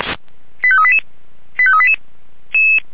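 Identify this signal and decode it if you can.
Five-tone selective call (ZVEI), heard as quick runs of five tones, tones of about 70 ms. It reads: 85388 85388
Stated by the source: ZVEI five-tone selective call 85388 received on SDR.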